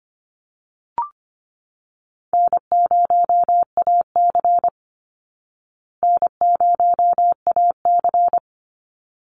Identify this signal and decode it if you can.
Morse code, heard as a keyed tone, about 25 wpm, 700 Hz. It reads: N0AC N0AC
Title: Easy Call Signs Repeated 2x - 25wpm